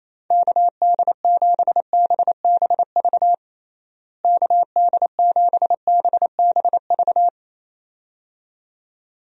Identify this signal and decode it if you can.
Morse code, heard as a keyed tone, about 28 wpm, 700 Hz. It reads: KD7BBV KD7BBV
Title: USA Call Signs - 6 Characters - Repeated 2x - 28wpm